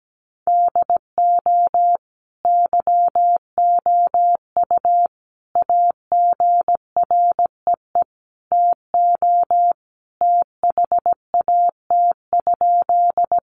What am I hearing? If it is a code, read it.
Morse code, heard as a keyed tone, about 17 wpm, 700 Hz. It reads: DO YOU AGREE TO THAT?
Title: Sentences from Top 1100 Words 17wpm